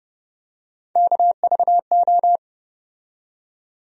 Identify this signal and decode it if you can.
Morse code, heard as a keyed tone, about 30 wpm, 700 Hz. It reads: KVO
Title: USA Call Sign Postfixes - 3 Letters 30wpm